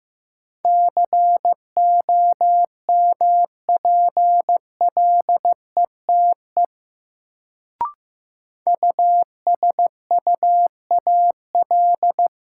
Morse code, heard as a keyed tone, about 16 wpm, 700 Hz. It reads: COMPLETE USUAL
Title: Sentences from Top 400 Words - New Word Drill - 15wpm